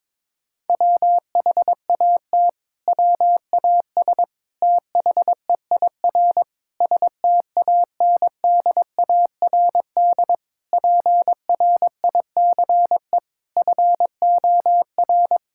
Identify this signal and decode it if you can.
Morse code, heard as a keyed tone, about 22 wpm, 700 Hz. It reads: WHAT WAS THEIR STANDARD PRICE FOR
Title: Sentences from Top 1000 Words - Review Part 6 - 22wpm